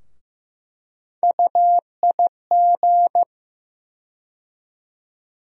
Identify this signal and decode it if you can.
Morse code, heard as a keyed tone, about 15 wpm, 700 Hz. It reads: UIG
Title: USA Call Sign Postfixes - 3 Letters 15wpm